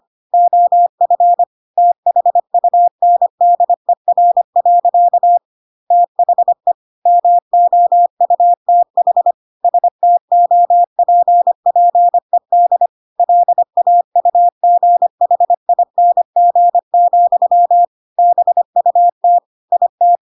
Morse code, heard as a keyed tone, about 25 wpm, 700 Hz. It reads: OF THUNDER. THE MOUTH STOPPED LAUGHING, BUT IT
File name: Pinocchio / Pinocchio ch3 25 25 ik1hge